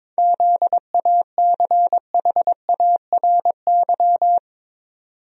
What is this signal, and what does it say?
Morse code, heard as a keyed tone, about 22 wpm, 700 Hz. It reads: ZACHARY
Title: USA Common Names - 22wpm